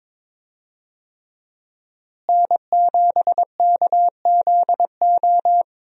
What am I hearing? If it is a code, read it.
Morse code, heard as a keyed tone, about 22 wpm, 700 Hz. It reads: N7KZO
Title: USA Call Signs - 5 Characters - Repeated 3x - 22wpm